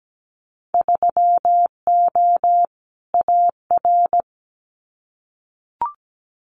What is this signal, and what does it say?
Morse code, heard as a keyed tone, about 17 wpm, 700 Hz. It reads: 3O AR
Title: Winter Field Day Exchanges Encore 17wpm